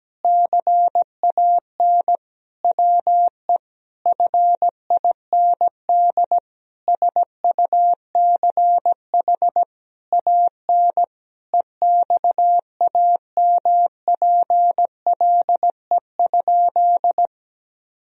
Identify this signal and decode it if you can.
Morse code, heard as a keyed tone, about 17 wpm, 700 Hz. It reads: CAN WE FIND SUCH AN EXAMPLE?
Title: Sentences from Top 1000 Words - Review Part 1 - 17wpm